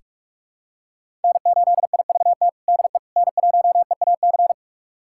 Morse code, heard as a keyed tone, about 46 wpm, 700 Hz. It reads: N8IVT DE N1EAC
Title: USA Callsign DE Callsign - 5 Characters - 45wpm